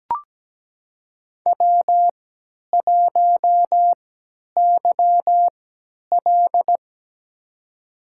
Morse code, heard as a keyed tone, about 17 wpm, spaced as for 10 wpm, 700 Hz. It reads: W1YL
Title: USA Call Signs - 4 Characters - 3x Character Spacing - 17wpm